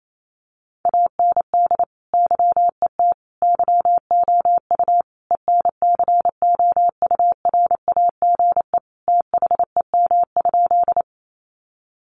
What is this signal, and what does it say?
Morse code, heard as a keyed tone, about 28 wpm, 700 Hz. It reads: AND YET YOU ENCOURAGE THEM?
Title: Sentences from Top 1400 Words 28wpm